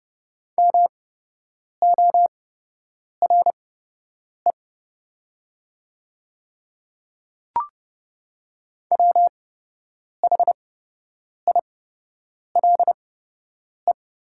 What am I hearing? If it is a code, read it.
Morse code, heard as a keyed tone, about 30 wpm, spaced as for 8 wpm, 690 Hz. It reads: MORE WHILE
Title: 8x Character Spacing - Top 300 Words - 30wpm